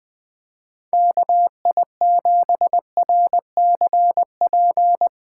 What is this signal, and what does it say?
Morse code, heard as a keyed tone, about 20 wpm, 700 Hz. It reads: KI7RCP
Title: USA Callsign DE Callsign 20wpm